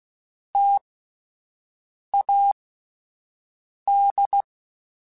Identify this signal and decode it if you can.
Morse code, heard as a keyed tone, about 16 wpm, spaced as for 5 wpm, 800 Hz. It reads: TAD
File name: Alphabetical / Alphabetical mp3s / Ts @5